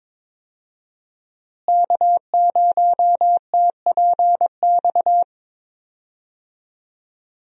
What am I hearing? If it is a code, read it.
Morse code, heard as a keyed tone, about 22 wpm, 700 Hz. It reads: K0TPX
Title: USA Call Signs - 5 Characters - Repeated 2x - 22wpm